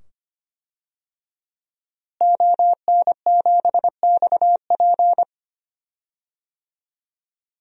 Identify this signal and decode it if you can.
Morse code, heard as a keyed tone, about 25 wpm, 700 Hz. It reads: ON7XP
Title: DXCC-100 Call Signs - 5 Characters - Repeated 2x - 25wpm